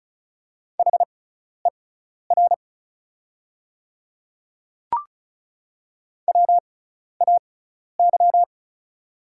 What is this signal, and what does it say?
Morse code, heard as a keyed tone, about 36 wpm, spaced as for 12 wpm, 700 Hz. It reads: HER WAY
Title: 6x Character Spacing - Top 300 Words - 35wpm